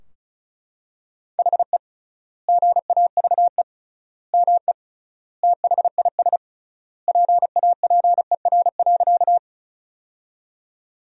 Morse code, heard as a keyed tone, about 34 wpm, 700 Hz. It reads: HE GAVE ME THIS PAPER.
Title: 3x Word-Spacing - Sentences from Top 500 Words - 35wpm